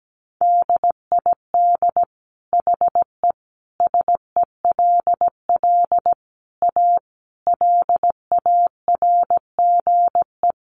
Morse code, heard as a keyed tone, about 17 wpm, 700 Hz. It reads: DID HE SELL A LARGE
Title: Sentences from Top 1300 Words 17wpm